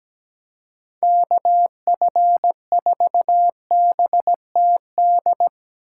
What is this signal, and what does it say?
Morse code, heard as a keyed tone, about 17 wpm, 700 Hz. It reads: KF4BTD
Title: USA Call Signs - 3x Character Spacing - 17wpm